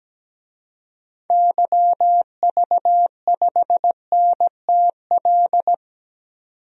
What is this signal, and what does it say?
Morse code, heard as a keyed tone, about 17 wpm, 700 Hz. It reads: YV5NTL
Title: DXCC-100 Call Signs - 6 Characters - 17wpm